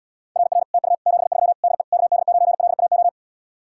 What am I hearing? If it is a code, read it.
Morse code, heard as a keyed tone, about 79 wpm, 700 Hz. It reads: RR TU 73 DE KD9LAO